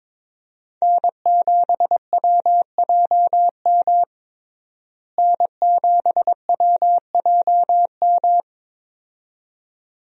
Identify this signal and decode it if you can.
Morse code, heard as a keyed tone, about 22 wpm, 700 Hz. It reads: N7WJM N7WJM
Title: USA Call Signs - 5 Characters - Repeated 2x - 22wpm